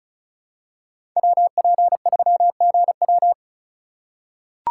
Morse code, heard as a keyed tone, about 36 wpm, 690 Hz. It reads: WP3GW
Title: USA Call Signs - 5 Characters - 3x Character Spacing - 35wpm